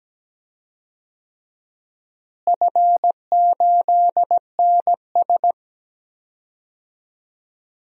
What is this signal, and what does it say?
Morse code, heard as a keyed tone, about 17 wpm, 700 Hz. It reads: F8NS
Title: DXCC-100 Call Signs - 4 Characters - Repeated 2x - 17wpm